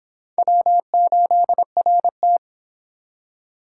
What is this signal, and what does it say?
Morse code, heard as a keyed tone, about 26 wpm, 700 Hz. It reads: W8RT